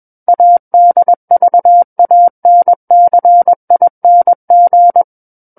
Morse code, heard as a keyed tone, about 21 wpm, 700 Hz. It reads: ADVANCING
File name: Book One / 12-Chapter 12 - 21 WPM